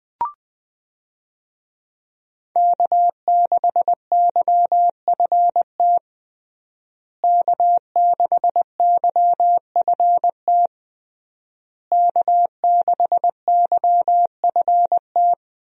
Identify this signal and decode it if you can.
Morse code, heard as a keyed tone, about 20 wpm, 700 Hz. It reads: K6YFT K6YFT K6YFT
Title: USA Call Signs - 5 Characters - Repeated 3x - 20wpm